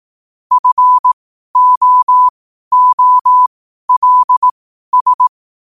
Morse code, heard as a keyed tone, about 18 wpm, spaced as for 12 wpm, 1000 Hz. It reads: FOOLS
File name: qotdfiles / 12wpm